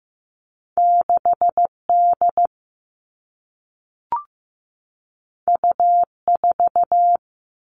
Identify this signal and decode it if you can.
Morse code, heard as a keyed tone, about 15 wpm, 700 Hz. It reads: 6 D U 4
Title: Two Challenging Letter-Number Combinations 15wpm